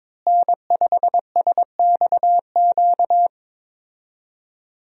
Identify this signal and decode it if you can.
Morse code, heard as a keyed tone, about 21 wpm, 700 Hz. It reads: N5SXQ
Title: USA Call Signs - 5 Characters - Repeated 2x - 22wpm